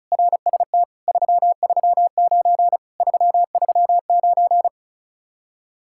Morse code, heard as a keyed tone, about 43 wpm, 700 Hz. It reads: RST 339 339
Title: Speed-Racing Common QSO Phrases - 35wpm